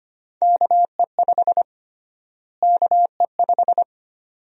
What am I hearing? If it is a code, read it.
Morse code, heard as a keyed tone, about 25 wpm, 700 Hz. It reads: KE5 KE5
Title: USA Call Sign Prefixes - Repeated 2x - 25wpm